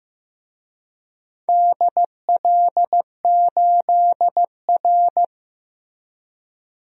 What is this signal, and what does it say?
Morse code, heard as a keyed tone, about 15 wpm, 700 Hz. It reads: DL8R
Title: DXCC-100 Call Signs - 4 Characters - Repeated 2x - 15wpm